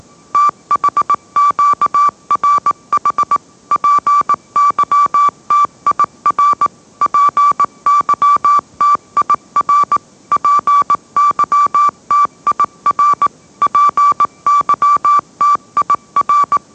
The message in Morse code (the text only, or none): THQRH PYTIR PYTIR PYTIR PYTIR